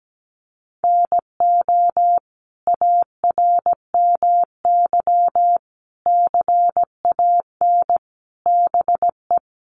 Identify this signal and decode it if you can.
Morse code, heard as a keyed tone, about 17 wpm, 700 Hz. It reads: NO ARMY CAN BE
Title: Sentences from Top 1300 Words 17wpm